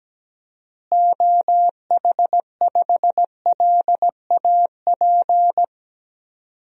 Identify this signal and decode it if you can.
Morse code, heard as a keyed tone, about 17 wpm, 700 Hz. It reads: OH5LAP
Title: DXCC-100 Call Signs - 6 Characters - Repeated 3x - 17wpm